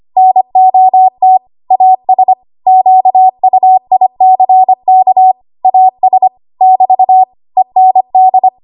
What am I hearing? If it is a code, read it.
Morse code, heard as a keyed tone, about 25 wpm, 750 Hz. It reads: NOT AS QUICK AS <BT> END
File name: Archive / 25 WPM / 25WPM